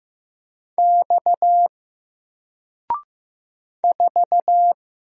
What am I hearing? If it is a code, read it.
Morse code, heard as a keyed tone, about 15 wpm, 700 Hz. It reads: X 4
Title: Single Character (Letter:Number:Punctuation:Prosign) 15wpm